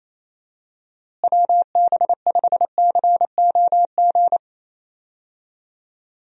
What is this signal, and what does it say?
Morse code, heard as a keyed tone, about 28 wpm, 700 Hz. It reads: WB5COG
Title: USA Call Signs - 6 Characters - Repeated 3x - 28wpm